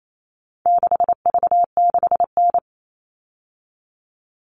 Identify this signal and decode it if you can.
Morse code, heard as a keyed tone, about 28 wpm, 700 Hz. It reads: 6V6N
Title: All Cut Numbers - 4-Digit 28wpm